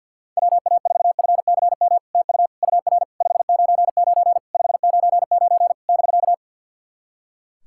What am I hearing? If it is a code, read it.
Morse code, heard as a keyed tone, about 48 wpm, 700 Hz. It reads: WA4FCM TU UR 599 599 <BK>
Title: SOTA QSO Practice 50wpm